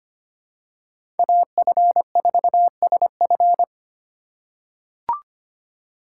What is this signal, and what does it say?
Morse code, heard as a keyed tone, about 25 wpm, 700 Hz. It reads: AF4SF
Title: USA Call Signs - 5 Characters - 8x Character Spacing - 25wpm